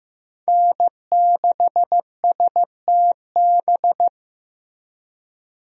Morse code, heard as a keyed tone, about 15 wpm, 700 Hz. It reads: N6STB
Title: USA Call Signs - 5 Characters - Repeated 2x - 15wpm